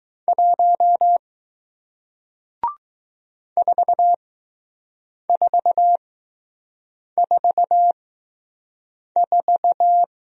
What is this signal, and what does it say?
Morse code, heard as a keyed tone, about 20 wpm, spaced as for 6 wpm, 700 Hz. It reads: 1 4444